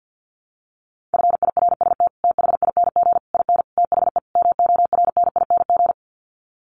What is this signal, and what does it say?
Morse code, heard as a keyed tone, about 50 wpm, 700 Hz. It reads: FIRST THING IN THE MORNING
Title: Sets of 5 Words Encore2 50wpm